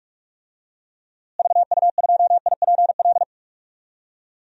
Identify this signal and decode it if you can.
Morse code, heard as a keyed tone, about 46 wpm, 700 Hz. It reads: VU2IPL
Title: DXCC-100 Call Signs - 6 Characters - Repeated 3x - 45wpm